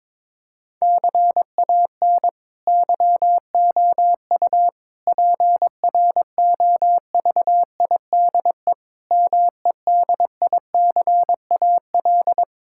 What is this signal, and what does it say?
Morse code, heard as a keyed tone, about 22 wpm, 700 Hz. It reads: CAN YOU PROVIDE MEDICAL
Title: Sentences from Top 1000 Words - Review Part 5 - 22wpm